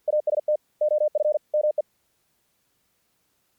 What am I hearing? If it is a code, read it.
Morse code, heard as a keyed tone, about 50 wpm, 600 Hz. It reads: AST OF ME
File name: archived / CRSnet